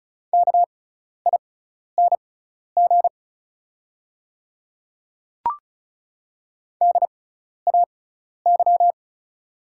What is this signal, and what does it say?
Morse code, heard as a keyed tone, about 36 wpm, spaced as for 12 wpm, 700 Hz. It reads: KING DAY
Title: 6x Character Spacing - Top 500 Words - 35wpm